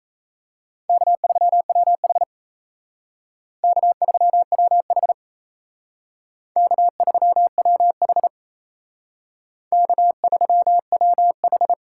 Morse code, heard as a keyed tone, about 37 wpm, 700 Hz. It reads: K3WH K3WH K3WH K3WH